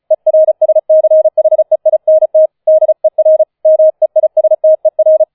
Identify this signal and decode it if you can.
Morse code, heard as a keyed tone, about 36 wpm, 630 Hz. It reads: ERSCHEINT DER MEISTER